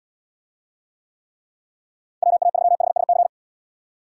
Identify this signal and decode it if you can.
Morse code, heard as a keyed tone, about 74 wpm, 700 Hz. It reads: WA3HSC